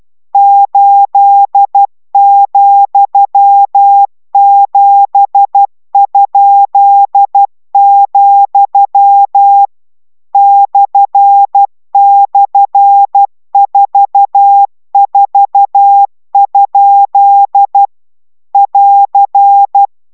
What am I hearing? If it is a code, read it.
Morse code, heard as a keyed tone, about 12 wpm, 800 Hz. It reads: 8,7?, //44? <AR>